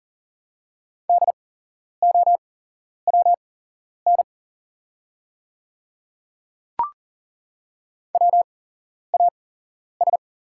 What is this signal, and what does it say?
Morse code, heard as a keyed tone, about 40 wpm, spaced as for 11 wpm, 700 Hz. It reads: DOWN WAS
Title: 8x Character Spacing - Top 200 Words - 40wpm